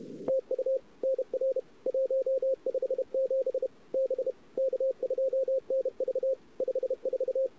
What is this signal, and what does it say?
Morse code, heard as a keyed tone, about 31 wpm, 530 Hz. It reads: TU NR 157 B K2NV 54
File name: CW / 2008-ss-cw-k5tr-op-n5rz -14:00